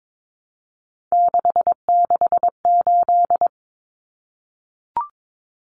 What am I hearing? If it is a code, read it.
Morse code, heard as a keyed tone, about 22 wpm, 700 Hz. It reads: 668
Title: Common Cut Numbers - 3-Digit 22wpm